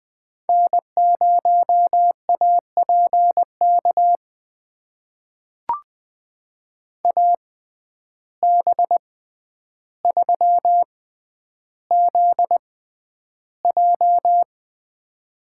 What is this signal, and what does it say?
Morse code, heard as a keyed tone, about 20 wpm, spaced as for 10 wpm, 700 Hz. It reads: N0APK AB3ZJ
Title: USA Call Signs - 5 Characters - 6x Character Spacing - 20wpm